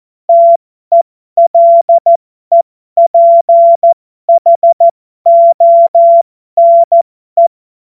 Morse code, heard as a keyed tone, about 14 wpm, spaced as for 12 wpm, 680 Hz. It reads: TELEPHONE
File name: Morsekurs / Witze